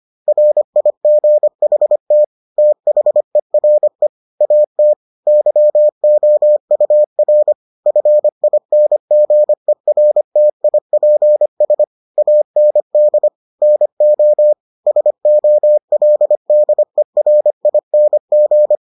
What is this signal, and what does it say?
Morse code, heard as a keyed tone, about 25 wpm, 600 Hz. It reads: RIGHT THERE AT YOUR FINGERTIPS AND NO SOLDERING